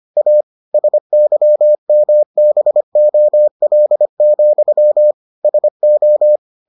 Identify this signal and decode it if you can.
Morse code, heard as a keyed tone, about 25 wpm, 600 Hz. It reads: A SYMBOL, SO